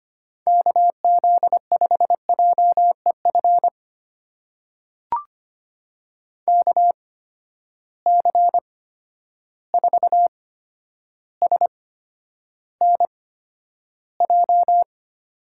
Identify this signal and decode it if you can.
Morse code, heard as a keyed tone, about 25 wpm, spaced as for 11 wpm, 700 Hz. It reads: KZ5JEF KC4SNJ
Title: USA Call Signs - 6 Characters - 8x Character Spacing - 25wpm